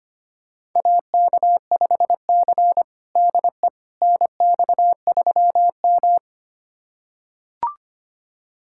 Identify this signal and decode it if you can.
Morse code, heard as a keyed tone, about 25 wpm, 700 Hz. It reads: AK5C DE NX3M
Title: USA Callsign DE Callsign - 4 Characters - 25wpm